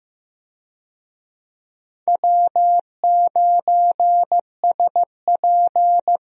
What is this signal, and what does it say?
Morse code, heard as a keyed tone, about 15 wpm, 700 Hz. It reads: W9SP
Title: USA Call Signs - 4 Characters - Repeated 3x - 15wpm